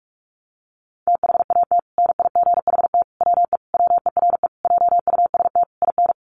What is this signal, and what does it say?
Morse code, heard as a keyed tone, about 44 wpm, 700 Hz. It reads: THAT NIGHT WE WERE JUST IN
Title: Sentences from Top 300 Words 45wpm